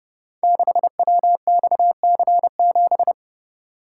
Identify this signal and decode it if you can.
Morse code, heard as a keyed tone, about 30 wpm, 700 Hz. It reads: 6WXC7
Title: Code Groups - 5 Letter:Numbers - 30wpm